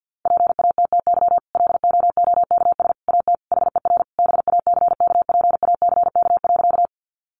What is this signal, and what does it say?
Morse code, heard as a keyed tone, about 49 wpm, 700 Hz. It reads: PATTY LOOKS AT HER BACKPACK.
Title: Additional Easy Phrases 50wpm